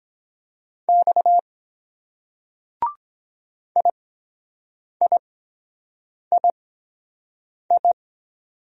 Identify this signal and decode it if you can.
Morse code, heard as a keyed tone, about 22 wpm, spaced as for 7 wpm, 700 Hz. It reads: X IIII